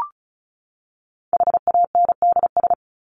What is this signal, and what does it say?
Morse code, heard as a keyed tone, about 34 wpm, 700 Hz. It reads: HANDS
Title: ES:S Suffix - Top 100 Words with Base Words 35wpm